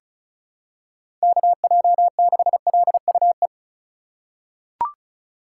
Morse code, heard as a keyed tone, about 36 wpm, 700 Hz. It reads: KJ6LUE
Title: USA Call Signs - 6 Characters - 3x Character Spacing - 35wpm